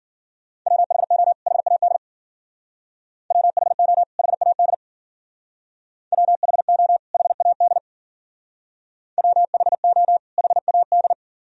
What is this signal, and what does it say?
Morse code, heard as a keyed tone, about 49 wpm, 700 Hz. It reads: WHO HAD WHO HAD WHO HAD WHO HAD
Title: Speed-Racing - Sets of 2 Words 40wpm